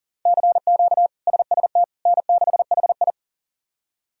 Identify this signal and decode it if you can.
Morse code, heard as a keyed tone, about 40 wpm, 700 Hz. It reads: CQ SST N6HI
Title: K1USN SST QSO Practice - 40wpm